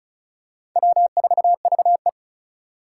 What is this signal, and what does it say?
Morse code, heard as a keyed tone, about 36 wpm, 700 Hz. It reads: W4VE
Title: Four Challenging Letter-Number Combinations 35wpm